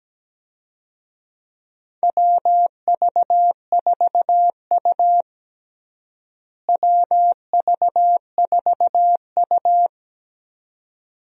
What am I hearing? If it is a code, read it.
Morse code, heard as a keyed tone, about 17 wpm, 700 Hz. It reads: WV4U WV4U
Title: USA Call Signs - 4 Characters - Repeated 2x - 17wpm